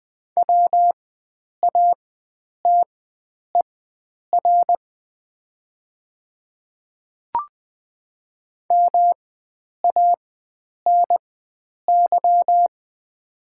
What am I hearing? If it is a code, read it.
Morse code, heard as a keyed tone, about 20 wpm, spaced as for 9 wpm, 700 Hz. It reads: WATER MANY
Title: 4x Character Spacing - Top 100 Words - 20wpm